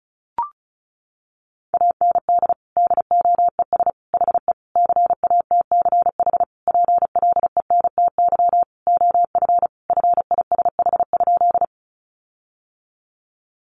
Morse code, heard as a keyed tone, about 34 wpm, 700 Hz. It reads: AND DOES HE CATCH PLENTY OF FISH?
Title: Sentences from Top 1500 Words 35wpm